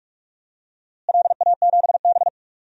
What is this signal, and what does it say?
Morse code, heard as a keyed tone, about 46 wpm, 700 Hz. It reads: LA7B
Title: DXCC-100 Call Signs - 4 Characters - Repeated 2x - 45wpm